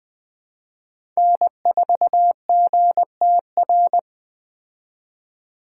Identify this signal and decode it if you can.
Morse code, heard as a keyed tone, about 21 wpm, 700 Hz. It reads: N4GTR
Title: Speed Racing - USA 1x3 Call Signs 17wpm